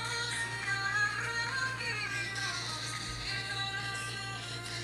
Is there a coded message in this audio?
Yes: Emergency Alert System (SAME) tones from 0.6 to 4.0 s, heard as harsh buzzing bursts.